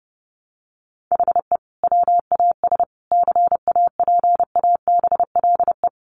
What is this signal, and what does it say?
Morse code, heard as a keyed tone, about 30 wpm, 700 Hz. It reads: HE WAS CAPABLE
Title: Sentences from Top 1200 Words 30wpm